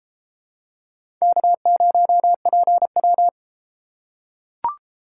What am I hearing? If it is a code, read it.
Morse code, heard as a keyed tone, about 32 wpm, 700 Hz. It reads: K0PW